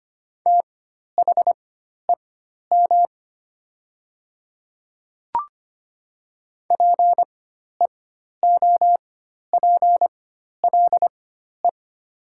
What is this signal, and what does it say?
Morse code, heard as a keyed tone, about 25 wpm, spaced as for 12 wpm, 700 Hz. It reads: THEM PEOPLE
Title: 4x Character Spacing - Top 200 Words - 25wpm